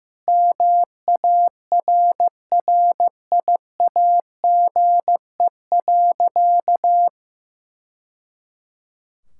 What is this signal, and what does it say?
Morse code, heard as a keyed tone, about 15 wpm, 700 Hz. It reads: MARRIAGE.
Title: Sentences from Top 1000 Words - Review Part 5 - 15wpm